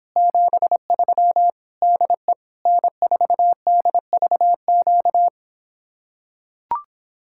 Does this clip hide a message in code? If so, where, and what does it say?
Yes: Morse code from 0.2 to 5.3 s, heard as a keyed tone, about 26 wpm, 700 Hz. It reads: 73 DE N4DVQ